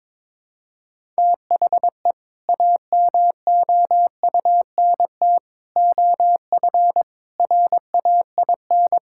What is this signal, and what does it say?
Morse code, heard as a keyed tone, about 22 wpm, 700 Hz. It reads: THE AMOUNT OF RAIN
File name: practice / 22wpm / Sentences with 3-Word Phrases - 22wpm